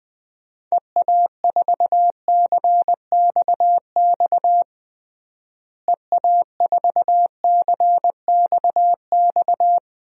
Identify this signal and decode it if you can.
Morse code, heard as a keyed tone, about 20 wpm, 700 Hz. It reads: EA4CXX EA4CXX
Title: DXCC-100 Call Signs - 6 Characters - Repeated 3x - 20wpm